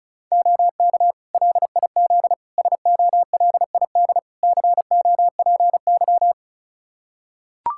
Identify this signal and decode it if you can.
Morse code, heard as a keyed tone, about 34 wpm, 700 Hz. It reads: OK LIZ SOLID COPY
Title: K1USN SST QSO Practice 35wpm